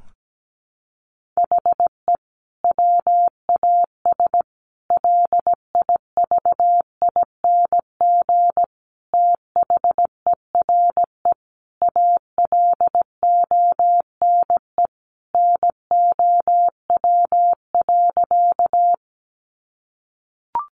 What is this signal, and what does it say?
Morse code, heard as a keyed tone, about 17 wpm, 700 Hz. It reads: HE WAS LIVING THERE ALONE NOW.
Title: Sentences from Top 1100 Words 17wpm